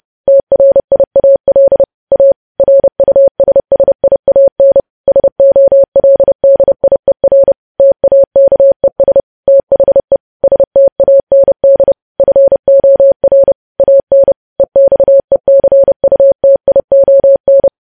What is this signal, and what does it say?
Morse code, heard as a keyed tone, about 30 wpm, 550 Hz. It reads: TRIAL A RUSSIAN SOLDIER TAKES THE STAND FOR AN EXECUTION
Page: news headlines in morse code at 30 wpm